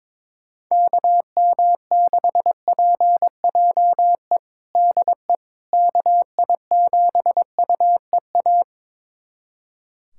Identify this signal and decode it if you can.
Morse code, heard as a keyed tone, about 22 wpm, 700 Hz. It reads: KM6PJE DE KI7UEA
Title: USA Callsign DE Callsign 22wpm